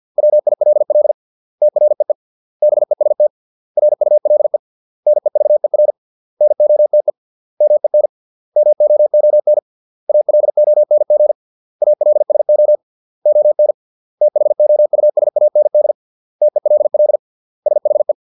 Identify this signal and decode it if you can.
Morse code, heard as a keyed tone, about 50 wpm, 600 Hz. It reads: WILL TREE BEST RULE NEVER NOTE MEN MOON ALONG ALSO ON THOUSAND TELL SHE